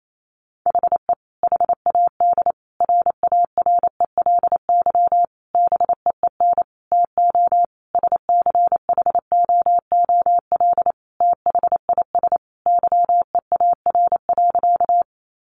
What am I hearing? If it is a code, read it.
Morse code, heard as a keyed tone, about 28 wpm, 700 Hz. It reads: HE HAD RARELY BEEN TO SCHOOL THIS YEAR.
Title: Sentences from Top 1400 Words 28wpm